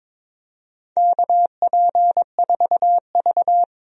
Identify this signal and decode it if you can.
Morse code, heard as a keyed tone, about 22 wpm, 700 Hz. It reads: KP4V
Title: DXCC-100 Call Signs - 4 Characters - 22wpm